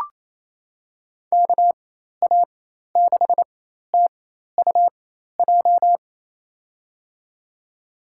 Morse code, heard as a keyed tone, about 28 wpm, spaced as for 13 wpm, 700 Hz. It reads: KA6TUJ
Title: USA Call Signs - 6 Characters - 4x Character Spacing - 28wpm